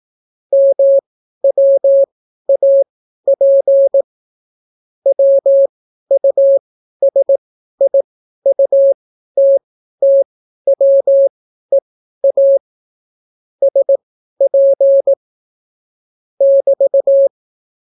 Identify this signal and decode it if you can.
Morse code, heard as a keyed tone, about 18 wpm, spaced as for 12 wpm, 550 Hz. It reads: MWAP WUSIUTTWEA SP <BT>